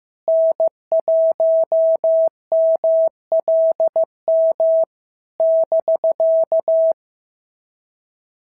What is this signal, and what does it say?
Morse code, heard as a keyed tone, about 14 wpm, 650 Hz. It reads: N1MLM <BK>
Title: SOTA QSO Practice 15wpm